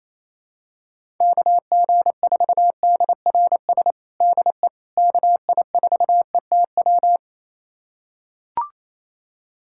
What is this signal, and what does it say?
Morse code, heard as a keyed tone, about 28 wpm, 700 Hz. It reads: KG4DRS DE KI4ETW
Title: USA Callsign DE Callsign - 6 Characters - 28wpm